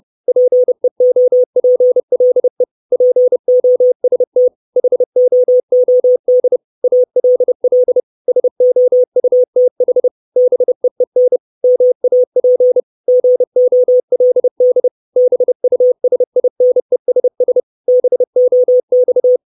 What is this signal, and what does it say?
Morse code, heard as a keyed tone, about 30 wpm, 500 Hz. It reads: PEOPLE POST HOOD ALL SOUTH BEEN MAP GOLD BUSINESS BOX
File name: 632 Words 30WPM